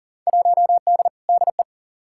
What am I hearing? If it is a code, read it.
Morse code, heard as a keyed tone, about 40 wpm, 700 Hz. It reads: 1D DE
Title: Field Day Exchanges - 40wpm